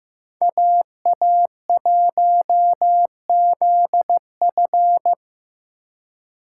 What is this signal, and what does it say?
Morse code, heard as a keyed tone, about 18 wpm, 700 Hz. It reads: AA1ZF
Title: Speed Racing - USA 2x2 Call Signs 15wpm